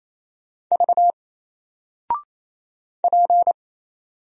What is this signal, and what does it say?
Morse code, heard as a keyed tone, about 28 wpm, 700 Hz. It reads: V P
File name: Single-Letter-Number-28wpm